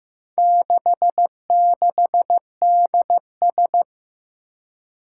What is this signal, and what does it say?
Morse code, heard as a keyed tone, about 15 wpm, 700 Hz. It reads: 66DS
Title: Four Challenging Letter-Number Combinations 15wpm